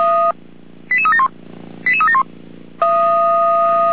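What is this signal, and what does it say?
Five-tone selective call (ZVEI), heard as quick runs of five tones, tones of about 70 ms. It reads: 88371 88371
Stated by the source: ZVEI five-tone selective call 88283 received on an SDR.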